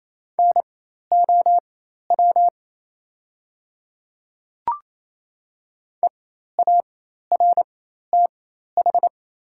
Morse code, heard as a keyed tone, about 28 wpm, spaced as for 13 wpm, 690 Hz. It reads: NOW EARTH
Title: 4x Character Spacing - Top 300 Words - 28wpm